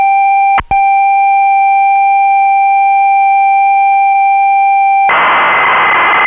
Computer data on cassette.